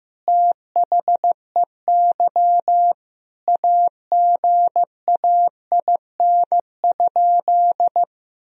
Morse code, heard as a keyed tone, about 15 wpm, 700 Hz. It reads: THEY AGAIN?
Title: Sentences from Top 1000 Words - Review Part 3 - 15wpm